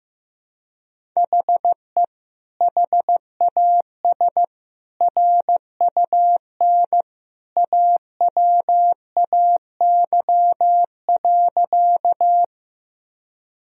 Morse code, heard as a keyed tone, about 15 wpm, 700 Hz. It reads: HE HAS RUN AWAY.
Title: Sentences from Top 1000 Words - Review Part 1 - 15wpm